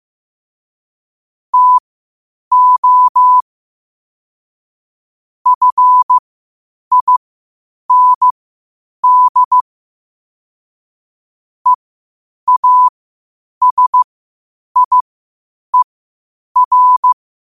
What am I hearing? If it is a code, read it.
Morse code, heard as a keyed tone, about 15 wpm, spaced as for 8 wpm, 1000 Hz. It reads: TO FIND EASIER
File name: qotdfiles / 8wpm